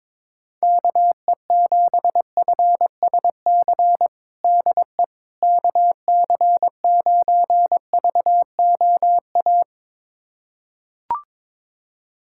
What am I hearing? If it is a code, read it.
Morse code, heard as a keyed tone, about 22 wpm, 700 Hz. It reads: KE7FSC DE KC9VOA
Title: USA Callsign DE Callsign - 6 Characters - 22wpm